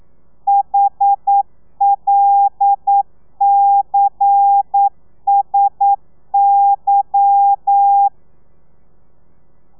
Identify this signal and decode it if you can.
Morse code, heard as a keyed tone, about 9 wpm, 800 Hz. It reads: HLCSY